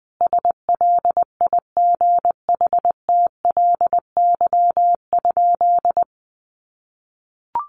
Morse code, heard as a keyed tone, about 20 wpm, 700 Hz. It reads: SLIGHTLY?
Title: Sentences from Top 1300 Words 20wpm